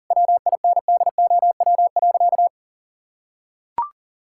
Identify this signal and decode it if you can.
Morse code, heard as a keyed tone, about 40 wpm, 700 Hz. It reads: WINDOW.
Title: Sentences from Top 1600 Words 40wpm